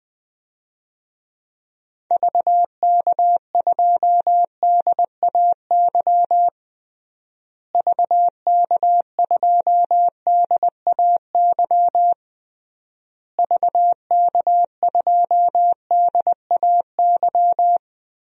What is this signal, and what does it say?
Morse code, heard as a keyed tone, about 19 wpm, 700 Hz. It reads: VK2DAY VK2DAY VK2DAY
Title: DXCC-100 Call Signs - 6 Characters - Repeated 3x - 20wpm